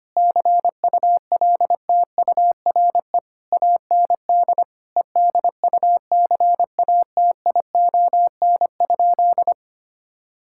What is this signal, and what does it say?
Morse code, heard as a keyed tone, about 25 wpm, 700 Hz. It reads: CULTURE AND EDUCATION?
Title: Sentences from Top 1000 Words - Review Part 6 - 25wpm